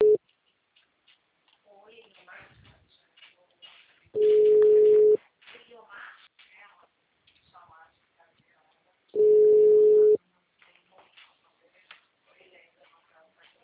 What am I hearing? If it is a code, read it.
Ringback tone (Europe, 425 Hz), with speech.